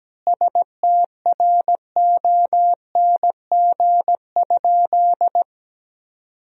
Morse code, heard as a keyed tone, about 17 wpm, 700 Hz. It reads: STRONG?
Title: 4x Word-Spacing - Sentences from Top 500 Words - 17wpm